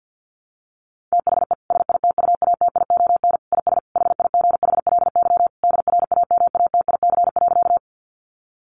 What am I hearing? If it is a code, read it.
Morse code, heard as a keyed tone, about 50 wpm, 700 Hz. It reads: THE SITUATION IS HIGHLY DRAMATIC.
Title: Sentences from Top 1300 Words 50wpm